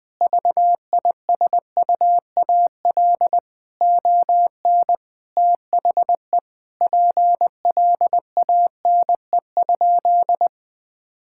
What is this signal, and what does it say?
Morse code, heard as a keyed tone, about 20 wpm, 700 Hz. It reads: VISUAL ON THE PLANE?
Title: Sentences from Top 1000 Words - Review Part 6 - 20wpm